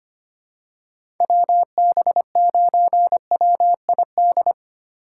Morse code, heard as a keyed tone, about 25 wpm, 700 Hz. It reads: WB9WID
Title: USA Call Signs - 4x Character Spacing - 25wpm